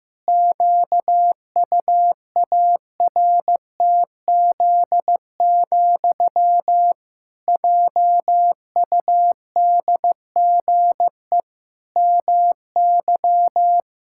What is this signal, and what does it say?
Morse code, heard as a keyed tone, about 15 wpm, 700 Hz. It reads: QUARTZ, JUDGE MY
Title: Pangram Challenge v2 - 15wpm